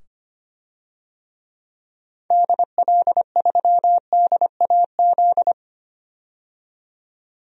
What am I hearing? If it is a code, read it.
Morse code, heard as a keyed tone, about 25 wpm, 700 Hz. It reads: DL3DAZ
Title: DXCC-100 Call Signs - 6 Characters - 25wpm